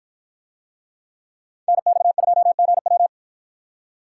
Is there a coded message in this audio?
Yes: Morse code from 1.7 to 3.1 s, heard as a keyed tone, about 53 wpm, 700 Hz. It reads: NX2GW